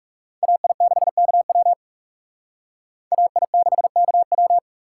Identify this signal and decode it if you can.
Morse code, heard as a keyed tone, about 36 wpm, 700 Hz. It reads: AI6KW AI6KW
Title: Speed Racing - USA 2x2 Call Signs 30wpm